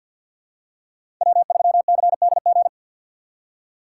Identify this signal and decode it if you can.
Morse code, heard as a keyed tone, about 51 wpm, 700 Hz. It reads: W3CDG
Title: USA Call Signs - 5 Characters - 6x Character Spacing - 50wpm